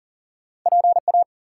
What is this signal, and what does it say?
Morse code, heard as a keyed tone, about 39 wpm, 700 Hz. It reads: PA
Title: US States and Canadian Provinces 40wpm